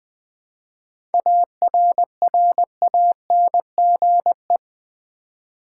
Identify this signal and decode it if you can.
Morse code, heard as a keyed tone, about 20 wpm, 700 Hz. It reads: ARRANGE
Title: 7 Letter Words - First 3 Letters Given - 20wpm